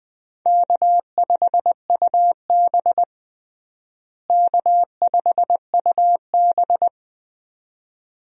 Morse code, heard as a keyed tone, about 20 wpm, 700 Hz. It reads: K5UB K5UB
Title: USA Call Signs - 4 Characters - Repeated 2x - 20wpm